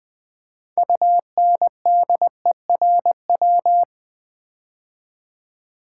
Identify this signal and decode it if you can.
Morse code, heard as a keyed tone, about 21 wpm, 700 Hz. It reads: UNDERW
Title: Build A Word - 3 Letters At A Time - 9-Letter Words - 20wpm